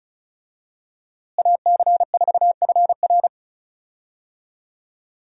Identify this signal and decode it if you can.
Morse code, heard as a keyed tone, about 36 wpm, 700 Hz. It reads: AC4FR
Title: USA Call Signs - 5 Characters - Repeated 2x - 35wpm